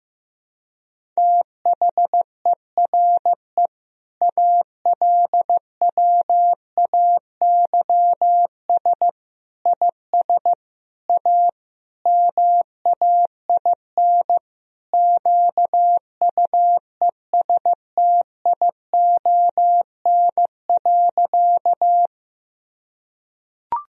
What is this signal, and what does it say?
Morse code, heard as a keyed tone, about 15 wpm, 700 Hz. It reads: THERE ALWAYS IS A MAIN QUESTION.
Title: Sentences from Top 1000 Words - Review Part 2 - 15wpm